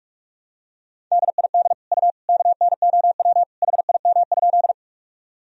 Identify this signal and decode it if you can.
Morse code, heard as a keyed tone, about 46 wpm, 700 Hz. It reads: DID U KNOW HIM?
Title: Sentences from Top 100 Words and Abbreviations 45wpm